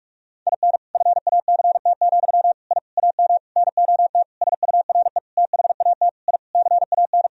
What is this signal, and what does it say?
Morse code, heard as a keyed tone, about 46 wpm, 700 Hz. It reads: IN FACT, I AM NOT SURE THAT I CAN
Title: Sentences from Top 1000 Words - Review Part 3 - 45wpm